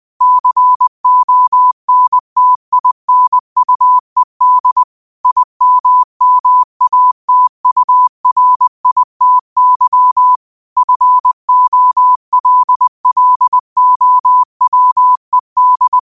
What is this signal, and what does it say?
Morse code, heard as a keyed tone, about 20 wpm, 1000 Hz. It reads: CONTINUED IMMATURITY FOLLOWED
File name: qotdfiles / 20wpm